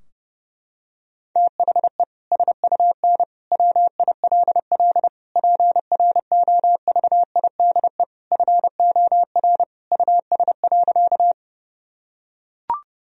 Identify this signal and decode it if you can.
Morse code, heard as a keyed tone, about 30 wpm, 700 Hz. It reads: THE SUN WILL PROVIDE FOR US.
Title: Sentences from Top 1000 Words - Review Part 3 - 30wpm